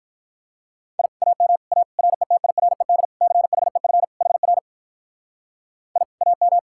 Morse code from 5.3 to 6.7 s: I AM

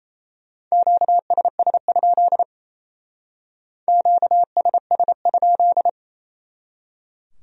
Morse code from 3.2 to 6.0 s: QSS?

